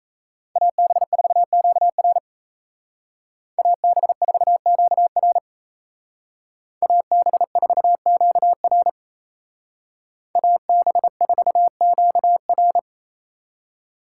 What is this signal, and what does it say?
Morse code, heard as a keyed tone, about 36 wpm, 700 Hz. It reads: AB4QR AB4QR AB4QR AB4QR